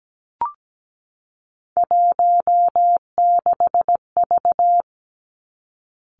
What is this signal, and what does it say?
Morse code, heard as a keyed tone, about 17 wpm, 700 Hz. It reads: 16V
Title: Three Challenging Letter-Number Combinations 17wpm